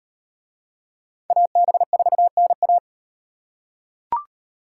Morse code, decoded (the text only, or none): AB4NA